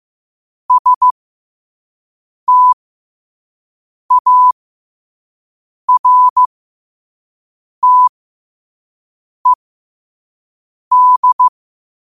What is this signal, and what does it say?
Morse code, heard as a keyed tone, about 15 wpm, spaced as for 5 wpm, 1000 Hz. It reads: STARTED
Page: New quotes every day in morse code at 5 Words per minute.